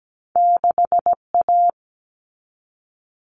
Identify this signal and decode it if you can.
Morse code, heard as a keyed tone, about 17 wpm, 700 Hz. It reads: 6 A
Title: All Cut Numbers - 2-Digit 17wpm